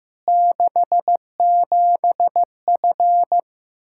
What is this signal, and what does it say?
Morse code, heard as a keyed tone, about 15 wpm, 700 Hz. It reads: 67F
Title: Practice QSO - Exchange 2 - Info - 15wpm